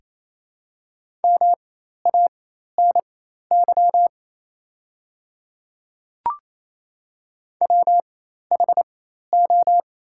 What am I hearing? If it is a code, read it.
Morse code, heard as a keyed tone, about 28 wpm, spaced as for 13 wpm, 700 Hz. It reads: MANY WHO
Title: 4x Character Spacing - Top 200 Words - 28wpm